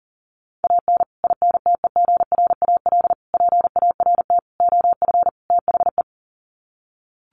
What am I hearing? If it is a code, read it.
Morse code, heard as a keyed tone, about 40 wpm, 700 Hz. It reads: AN INTEGRAL PART OF THE
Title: Sets of 5 Words 40wpm